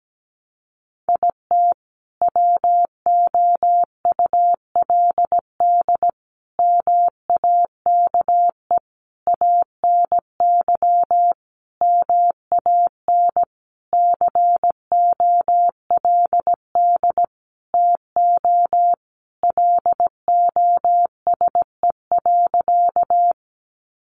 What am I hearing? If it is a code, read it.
Morse code, heard as a keyed tone, about 17 wpm, 700 Hz. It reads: IT WOULD MAKE ANY MAN COLD TO LOSE.
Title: Sentences from Top 1100 Words 17wpm